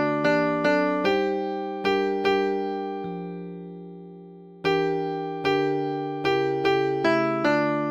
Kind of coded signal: none